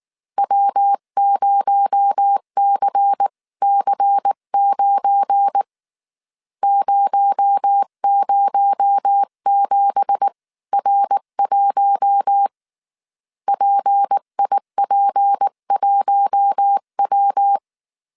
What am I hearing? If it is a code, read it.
Morse code, heard as a keyed tone, about 19 wpm, 800 Hz. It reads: W0CC9 007R1 PIP1W
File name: CW / DET / 1 9 1500C S B 18wpm 2551